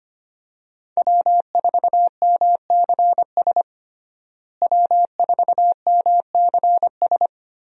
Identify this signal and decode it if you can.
Morse code, heard as a keyed tone, about 25 wpm, 700 Hz. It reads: W4MCS W4MCS
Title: USA Call Signs - 5 Characters - Repeated 3x - 25wpm